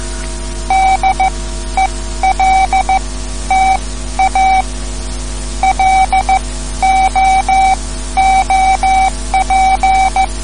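Morse code, heard as a keyed tone, about 15 wpm, spaced as for 12 wpm, 760 Hz. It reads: DELTA LOOP
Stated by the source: Morse code lesson, third lesson in a series for learning the Morse alphabet.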